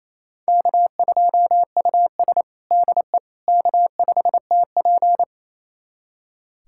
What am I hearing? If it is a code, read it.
Morse code, heard as a keyed tone, about 28 wpm, 700 Hz. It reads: K2US DE K5TP
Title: USA Callsign DE Callsign - 4 Characters - 28wpm